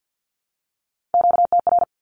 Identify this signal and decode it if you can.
Morse code, heard as a keyed tone, about 49 wpm, 700 Hz. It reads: QTR
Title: Q-Codes 50wpm